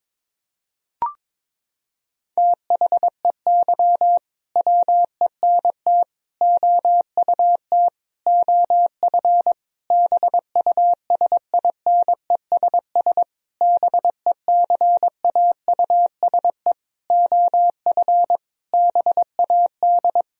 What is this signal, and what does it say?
Morse code, heard as a keyed tone, about 22 wpm, 700 Hz. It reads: THEY WENT OUT OF BUSINESS BECAUSE OF BAD
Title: Sentences from Top 1000 Words - Review Part 4 - 22wpm